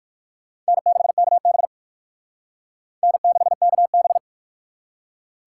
Morse code, heard as a keyed tone, about 56 wpm, 700 Hz. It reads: N6KB N6KB
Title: Speed Racing - USA Call Signs 45wpm